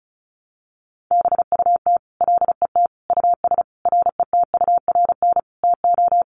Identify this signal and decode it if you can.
Morse code, heard as a keyed tone, about 34 wpm, 700 Hz. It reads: BUT LET US RETURN TO
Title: Sentences from Top 800 Words 35wpm